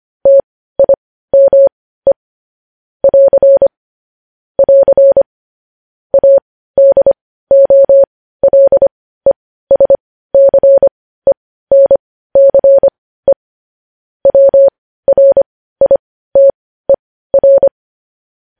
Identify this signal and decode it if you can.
Morse code, heard as a keyed tone, about 25 wpm, spaced as for 15 wpm, 550 Hz. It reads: TIME <AR> <AR> ADOLESCENCE WRITER
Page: News Headlines in Morse Code at 15 WPM